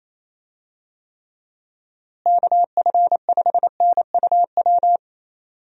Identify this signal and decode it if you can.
Morse code, heard as a keyed tone, about 28 wpm, 700 Hz. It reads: KF5NUW
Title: USA Call Signs - 6 Characters - Repeated 2x - 28wpm